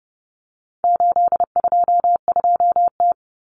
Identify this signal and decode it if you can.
Morse code, heard as a keyed tone, about 30 wpm, 700 Hz. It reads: 822T
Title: Common Cut Numbers - 4-Digit 30wpm